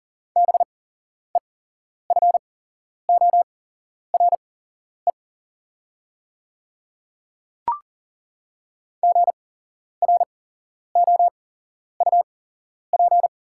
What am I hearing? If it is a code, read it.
Morse code, heard as a keyed tone, about 40 wpm, spaced as for 11 wpm, 700 Hz. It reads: BEFORE GROUP
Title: 8x Character Spacing - Top 300 Words - 40wpm